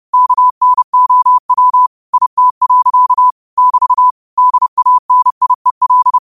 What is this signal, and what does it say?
Morse code, heard as a keyed tone, about 30 wpm, 1000 Hz. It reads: KNOW IT. <BT> DANIEL
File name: qotdfiles / 30wpm